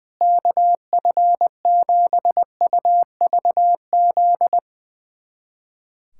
Morse code, heard as a keyed tone, about 20 wpm, 700 Hz. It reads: KF7UVZ
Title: USA Callsign DE Callsign - 6 Characters - 20wpm